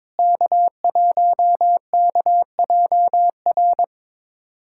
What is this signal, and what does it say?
Morse code, heard as a keyed tone, about 22 wpm, 700 Hz. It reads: K1KJR
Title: USA Callsign DE Callsign 22wpm